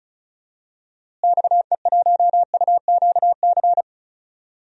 Morse code, heard as a keyed tone, about 36 wpm, 700 Hz. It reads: XE1UQC
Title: DXCC-100 Call Signs - 6 Characters - Repeated 3x - 35wpm